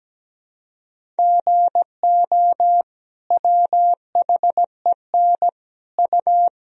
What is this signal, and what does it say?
Morse code, heard as a keyed tone, about 17 wpm, 700 Hz. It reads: GO WHEN U
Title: Sentences from Top 500 Words and Abbreviations 17wpm